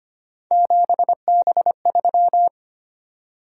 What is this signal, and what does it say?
Morse code, heard as a keyed tone, about 25 wpm, 700 Hz. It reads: 7B3
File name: Three Letter-Number Combinations - 25wpm